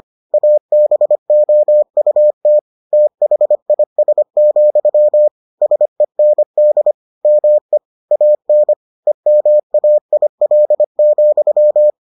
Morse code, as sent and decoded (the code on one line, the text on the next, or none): .- -... --- ..- - / - .... .. ... --..-- / ... -. -.. / -- . / .- -. / . -- .- .. .-.. --..--
ABOUT THIS, SEND ME AN EMAIL,